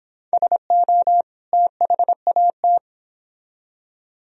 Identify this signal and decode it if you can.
Morse code, heard as a keyed tone, about 26 wpm, 700 Hz. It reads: SO THAT